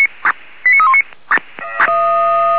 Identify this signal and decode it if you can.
Five-tone selective call (ZVEI), heard as quick runs of five tones, tones of about 80 ms. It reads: 89319